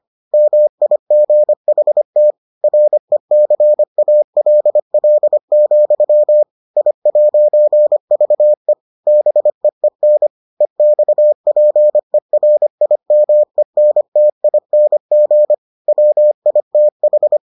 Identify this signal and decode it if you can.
Morse code, heard as a keyed tone, about 25 wpm, 600 Hz. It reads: MIGHT RECALL, I'VE BEEN EXPERIMENTING WITH